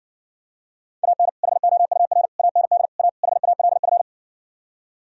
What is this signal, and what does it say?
Morse code, heard as a keyed tone, about 59 wpm, 700 Hz. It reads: AN HOUR AND A HALF